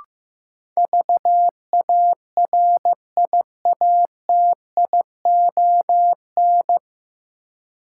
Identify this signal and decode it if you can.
Morse code, heard as a keyed tone, about 15 wpm, 700 Hz. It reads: VARIATION
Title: Sentences from Top 1600 Words - New Word Drill - 15wpm